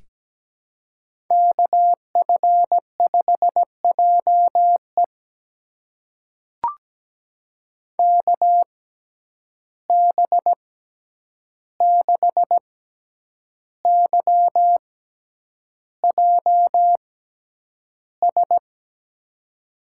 Morse code, heard as a keyed tone, about 17 wpm, spaced as for 8 wpm, 700 Hz. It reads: KF5JE KB6YJS
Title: USA Call Signs - 6x Character Spacing - 17wpm